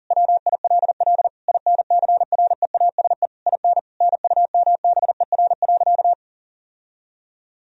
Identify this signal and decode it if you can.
Morse code, heard as a keyed tone, about 40 wpm, 700 Hz. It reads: WILL INCREASE IN NUMBER.
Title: Sentences from Top 1000 Words - Review Part 4 - 40wpm